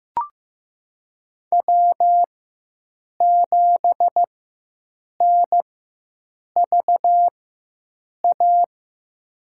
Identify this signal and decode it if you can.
Morse code, heard as a keyed tone, about 15 wpm, spaced as for 7 wpm, 700 Hz. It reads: W7NVA
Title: USA Call Signs - 4x Character Spacing - 15wpm